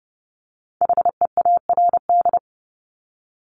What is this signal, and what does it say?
Morse code, heard as a keyed tone, about 30 wpm, 700 Hz. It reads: HEARD
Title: Multi-letter Phonemes Repeated 3x 30wpm